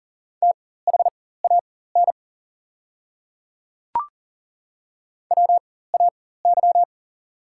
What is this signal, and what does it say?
Morse code, heard as a keyed tone, about 38 wpm, spaced as for 18 wpm, 690 Hz. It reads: THAN WAY
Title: 4x Character Spacing - Top 100 Words - 40wpm